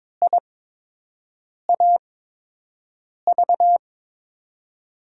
Morse code, heard as a keyed tone, about 22 wpm, spaced as for 6 wpm, 700 Hz. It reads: IAV
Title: USA Call Signs - 6 Characters - 8x Character Spacing - 22wpm